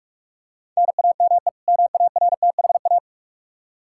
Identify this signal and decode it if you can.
Morse code, heard as a keyed tone, about 46 wpm, 700 Hz. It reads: NAME MARTHA
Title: Practice QSO - Exchange 1 - Beginning - 45wpm